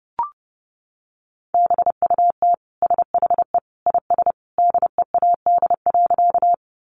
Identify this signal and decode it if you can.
Morse code, heard as a keyed tone, about 30 wpm, 700 Hz. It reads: BUT SHE IS DEAD.
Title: Sentences from Top 1500 Words 30wpm